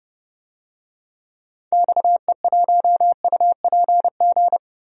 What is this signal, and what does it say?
Morse code, heard as a keyed tone, about 31 wpm, 700 Hz. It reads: XE1UPG
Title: DXCC-100 Call Signs - 6 Characters - 30wpm